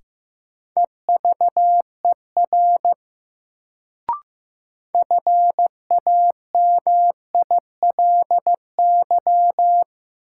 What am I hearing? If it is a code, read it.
Morse code, heard as a keyed tone, about 15 wpm, 700 Hz. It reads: EVER FAMILY
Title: Sentences from Top 400 Words - New Word Drill - 15wpm